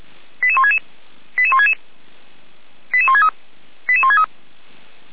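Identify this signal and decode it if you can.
Five-tone selective call (ZVEI), heard as quick runs of five tones, tones of about 70 ms. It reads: 88166 88166 88163 88163
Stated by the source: ZVEI five-tone call received by SDR.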